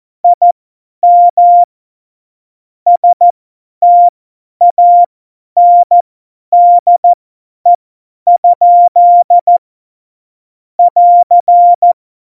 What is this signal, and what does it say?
Morse code, heard as a keyed tone, about 14 wpm, spaced as for 10 wpm, 700 Hz. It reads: IM STANDE? <AR>
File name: Morsekurs / Witze / Witz 20. August 2019, 50 F70